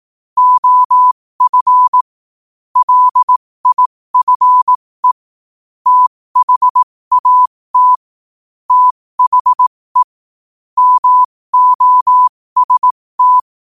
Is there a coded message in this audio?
Yes: Morse code from 0.4 to 13.4 s, heard as a keyed tone, about 18 wpm, spaced as for 15 wpm, 1000 Hz. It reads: OF LIFE THAT THE MOST